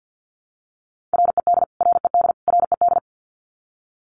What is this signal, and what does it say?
Morse code, heard as a keyed tone, about 50 wpm, 700 Hz. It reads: RED RED RED
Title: Top 500 Words Repeated 50wpm